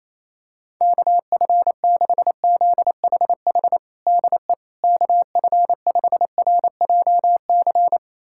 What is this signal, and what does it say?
Morse code, heard as a keyed tone, about 28 wpm, 700 Hz. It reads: KF6ZHH DE KF5RJC
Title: USA Callsign DE Callsign 28wpm